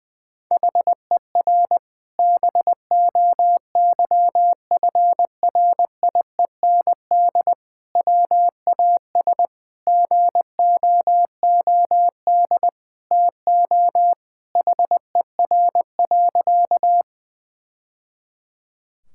Morse code, heard as a keyed tone, about 20 wpm, 700 Hz. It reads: HER BOYFRIEND WAS GOOD TO HER.
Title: Sentences from Top 1000 Words - Review Part 5 - 20wpm